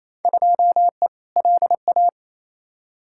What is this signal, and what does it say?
Morse code, heard as a keyed tone, about 28 wpm, 700 Hz. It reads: 2E LA
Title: Field Day Exchanges Encore - 28wpm